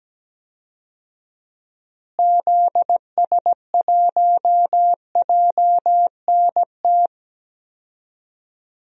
Morse code, heard as a keyed tone, about 17 wpm, 700 Hz. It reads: ZS1JNT
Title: DXCC-100 Call Signs - 6 Characters - Repeated 3x - 17wpm